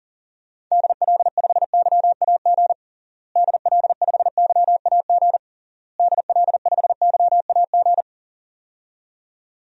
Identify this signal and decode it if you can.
Morse code, heard as a keyed tone, about 40 wpm, 700 Hz. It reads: DL5YAG DL5YAG DL5YAG
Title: DXCC-100 Call Signs - 6 Characters - Repeated 3x - 40wpm